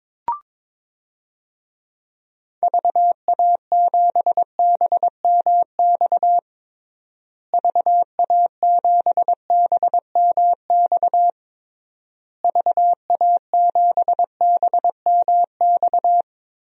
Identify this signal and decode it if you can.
Morse code, heard as a keyed tone, about 22 wpm, 700 Hz. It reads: VA7BMX VA7BMX VA7BMX
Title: DXCC-100 Call Signs - 6 Characters - Repeated 3x - 22wpm